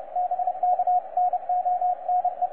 Morse code, heard as a keyed tone, about 30 wpm, 720 Hz, with radio noise. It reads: BK NONE
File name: posts / xw2cw-10-cw